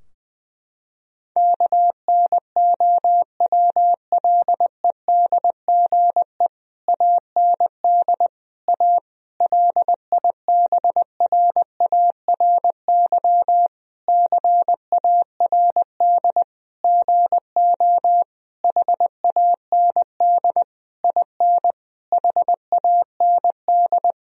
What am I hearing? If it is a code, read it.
Morse code, heard as a keyed tone, about 20 wpm, 700 Hz. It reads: KNOWLEDGE AND A LIBRARY CARD GO HAND IN HAND
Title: Sentences from Top 1000 Words - Review Part 4 - 20wpm